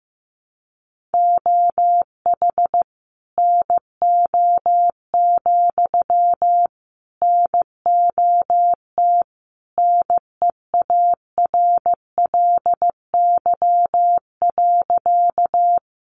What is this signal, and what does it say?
Morse code, heard as a keyed tone, about 15 wpm, 700 Hz. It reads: OH NO, NOT NEARLY.
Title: Sentences from Top 1300 Words 15wpm